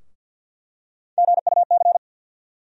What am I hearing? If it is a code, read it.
Morse code, heard as a keyed tone, about 47 wpm, 700 Hz. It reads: GUC
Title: USA Call Sign Postfixes - 3 Letters 50wpm